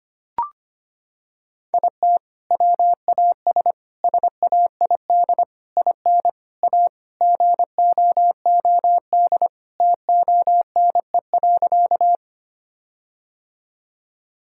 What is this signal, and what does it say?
Morse code, heard as a keyed tone, about 25 wpm, 700 Hz. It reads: IT WAS SAID IN A GOOD TONE.
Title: Sentences from Top 1600 Words 25wpm